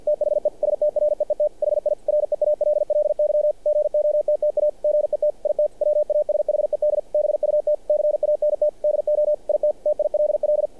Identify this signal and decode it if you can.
Morse code, heard as a keyed tone, about 48 wpm, 600 Hz. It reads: THE STREET HI REALLY GOTTA GET IT WASHED BUT CANT DO IT TILL
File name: archived / CRSnet